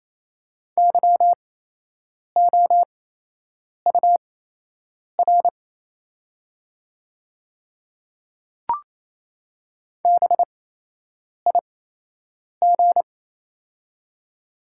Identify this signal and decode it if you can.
Morse code, heard as a keyed tone, about 28 wpm, spaced as for 8 wpm, 700 Hz. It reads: YOUR BIG
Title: 8x Character Spacing - Top 300 Words - 28wpm